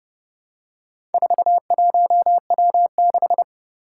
Morse code, heard as a keyed tone, about 30 wpm, 700 Hz. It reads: 41W6
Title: Four Challenging Letter-Number Combinations 30wpm